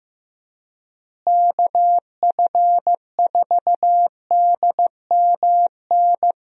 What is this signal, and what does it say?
Morse code, heard as a keyed tone, about 15 wpm, 700 Hz. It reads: KF4DMN
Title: USA Call Signs - 6 Characters - Repeated 3x - 15wpm